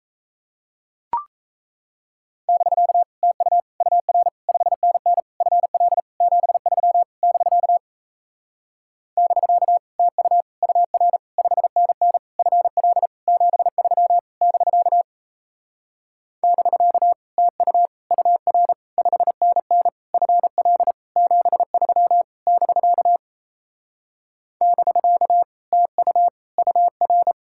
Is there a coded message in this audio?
Yes: Morse code from 2.5 to 27.3 s, heard as a keyed tone, about 38 wpm, 700 Hz. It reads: <BK> TU UR 5NN FL 73 <BK> <BK> TU UR 5NN FL 73 <BK> <BK> TU UR 5NN FL 73 <BK> <BK> TU UR